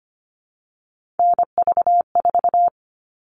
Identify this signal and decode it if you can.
Morse code, heard as a keyed tone, about 25 wpm, 700 Hz. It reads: NV4
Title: All Cut Numbers - 3-Digit 25wpm